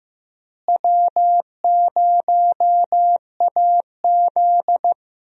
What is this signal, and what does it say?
Morse code, heard as a keyed tone, about 15 wpm, 700 Hz. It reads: W0AZ
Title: USA Call Signs - 4 Characters - Repeated 3x - 15wpm